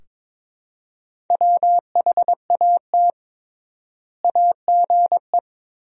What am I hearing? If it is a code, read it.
Morse code, heard as a keyed tone, about 22 wpm, 700 Hz. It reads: WHAT AGE
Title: 3x Word-Spacing - Sentences from Top 500 Words - 22wpm